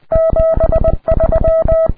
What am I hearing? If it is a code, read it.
Morse code, heard as a keyed tone, about 20 wpm, 650 Hz. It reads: 73
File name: ea5xq / qso ea5bm xe2 ea5xq qrp